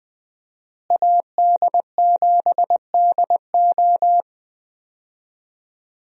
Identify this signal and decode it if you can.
Morse code, heard as a keyed tone, about 20 wpm, 700 Hz. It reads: AD7DO
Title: USA Call Signs - 5 Characters - Repeated 3x - 20wpm